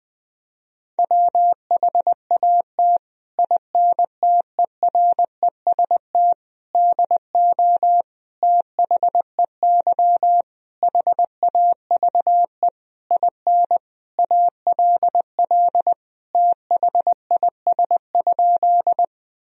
Morse code, heard as a keyed tone, about 20 wpm, 700 Hz. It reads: WHAT INTEREST DO THEY HAVE IN ALL THIS?
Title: Sentences from Top 1000 Words - Review Part 2 - 20wpm